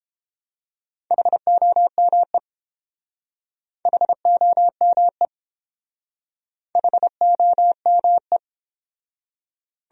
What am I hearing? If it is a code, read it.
Morse code, heard as a keyed tone, about 29 wpm, 700 Hz. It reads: HOME HOME HOME